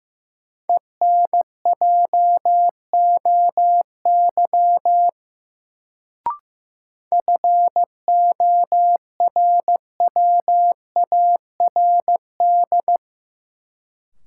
Morse code, heard as a keyed tone, about 15 wpm, 700 Hz. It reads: ENJOY FORWARD
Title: Sentences from Top 1300 Words - New Word Drill Alt - 15wpm